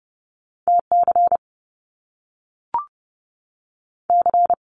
Morse code, heard as a keyed tone, about 30 wpm, 700 Hz. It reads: T C C
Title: Top Two Letter Combinations in Words 30wpm